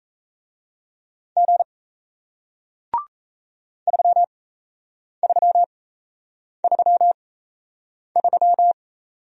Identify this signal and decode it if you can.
Morse code, heard as a keyed tone, about 38 wpm, spaced as for 8 wpm, 700 Hz. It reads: G 3333